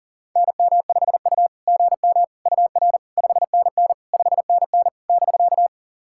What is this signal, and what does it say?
Morse code, heard as a keyed tone, about 40 wpm, 690 Hz. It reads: NM5U GM UR 5NN 5NN <BK>
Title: SOTA QSO Practice 40wpm